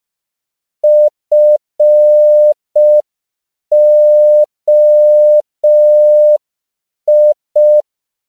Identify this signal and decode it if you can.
Morse code, heard as a keyed tone, about 5 wpm, 600 Hz. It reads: FOI
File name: CW / 5PPM